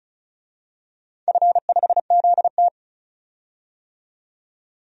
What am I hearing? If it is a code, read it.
Morse code, heard as a keyed tone, about 36 wpm, 700 Hz. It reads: F5ZT
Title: DXCC-100 Call Signs - 4 Characters - 35wpm